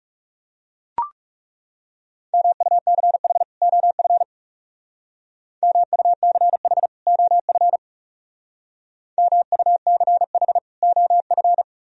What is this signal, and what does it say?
Morse code, heard as a keyed tone, about 37 wpm, 700 Hz. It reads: MUCH OF MUCH OF MUCH OF
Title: Speed-Racing - Sets of 2 Words 30wpm